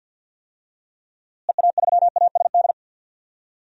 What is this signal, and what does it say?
Morse code, heard as a keyed tone, about 47 wpm, 700 Hz. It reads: EA3ASD
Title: DXCC-100 Call Signs - 6 Characters - Repeated 3x - 50wpm